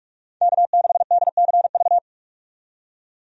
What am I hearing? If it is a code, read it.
Morse code, heard as a keyed tone, about 46 wpm, 700 Hz. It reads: K6DCV
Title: USA Call Signs - 5 Characters - 45wpm